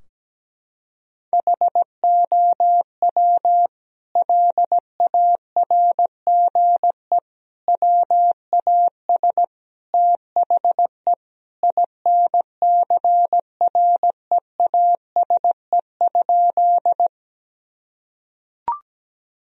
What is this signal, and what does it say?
Morse code, heard as a keyed tone, about 17 wpm, 700 Hz. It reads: HOW LARGE WAS THE INCREASE?
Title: Sentences from Top 1000 Words - Review Part 4 - 17wpm